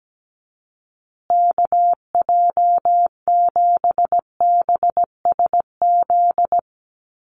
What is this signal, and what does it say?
Morse code, heard as a keyed tone, about 17 wpm, 700 Hz. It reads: KJ7BSZ
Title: USA Call Signs Repeated 2x 17wpm